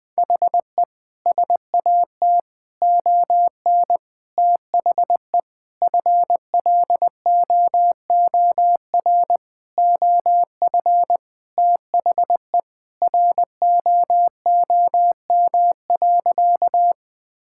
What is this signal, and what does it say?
Morse code, heard as a keyed tone, about 20 wpm, 700 Hz. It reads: HE SAT ON THE FLOOR OF THE ROOM.
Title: Sentences with 5-Word Phrases - 20wpm